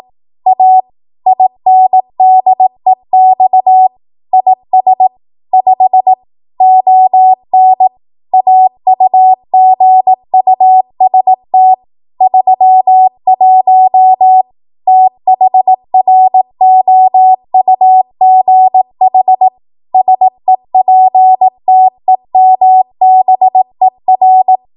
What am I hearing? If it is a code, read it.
Morse code, heard as a keyed tone, about 18 wpm, 750 Hz. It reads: A INDEX IS 5 ON AUGUST 31 THROUGH SEPTEMBER
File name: Archive / 18 WPM / 18WPM